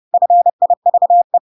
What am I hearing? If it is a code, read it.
Morse code, heard as a keyed tone, about 30 wpm, 700 Hz. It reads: FIVE
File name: Pinocchio / Pinocchio ch1 30 30 ik1hge